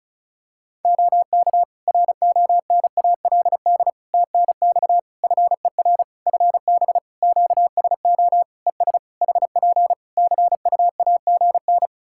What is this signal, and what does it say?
Morse code, heard as a keyed tone, about 34 wpm, 700 Hz. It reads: OK RONALD TNX FER FB QSO ES HP CUAGN
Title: Practice QSO - Exchange 4 - Ending - 35wpm